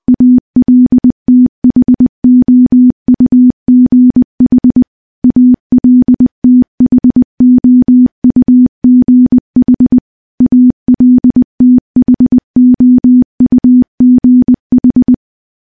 Morse code, heard as a keyed tone, about 21 wpm, 250 Hz. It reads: ALTHOUGH ALTHOUGH ALTHOUGH
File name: morse-training-20WPM